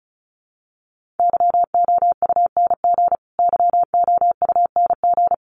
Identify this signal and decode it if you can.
Morse code, heard as a keyed tone, about 34 wpm, 700 Hz. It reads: YOUNG YOUNG
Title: Top 500 Words Repeated 35wpm